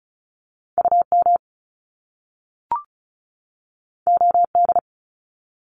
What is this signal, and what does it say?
Morse code, heard as a keyed tone, about 36 wpm, 700 Hz. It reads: UM OD
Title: Top Two Letter Combinations in Words 35wpm